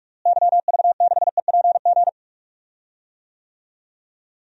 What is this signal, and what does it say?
Morse code, heard as a keyed tone, about 42 wpm, 700 Hz. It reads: YV6EPG
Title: DXCC-100 Call Signs - 6 Characters - Repeated 2x - 45wpm